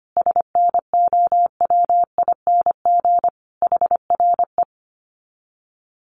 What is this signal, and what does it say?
Morse code, heard as a keyed tone, about 25 wpm, 700 Hz. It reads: SNOWING HRE
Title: Common QSO Phrases 25wpm